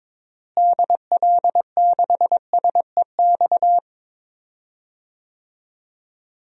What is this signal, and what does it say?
Morse code, heard as a keyed tone, about 22 wpm, 700 Hz. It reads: DL6SEX
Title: DXCC-100 Call Signs - 6 Characters - Repeated 2x - 22wpm